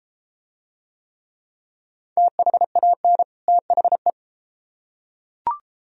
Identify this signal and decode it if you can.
Morse code, heard as a keyed tone, about 30 wpm, 700 Hz. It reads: THAN THE